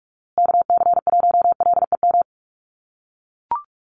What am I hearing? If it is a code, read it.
Morse code, heard as a keyed tone, about 44 wpm, 700 Hz. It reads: KC1LEM
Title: USA Call Signs Repeated 3x 45wpm